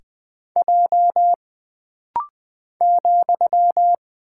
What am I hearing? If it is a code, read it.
Morse code, heard as a keyed tone, about 20 wpm, 700 Hz. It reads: J ,
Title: Single Character (Letter:Number:Punctuation:Prosign) - Rapid-Fire - 20wpm